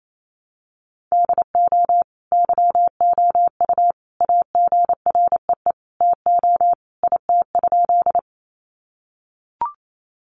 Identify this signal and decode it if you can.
Morse code, heard as a keyed tone, about 28 wpm, 700 Hz. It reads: DO YOU AGREE TO IT?
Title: Sentences from Top 1100 Words 28wpm